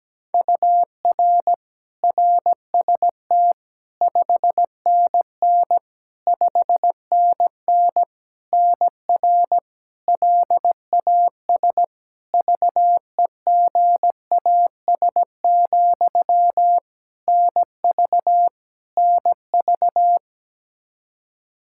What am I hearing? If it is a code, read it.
Morse code, heard as a keyed tone, about 17 wpm, 700 Hz. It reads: UR RST 5NN 5NN NR LAS VEGAS, NV NV
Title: K1USN SST QSO Practice 17wpm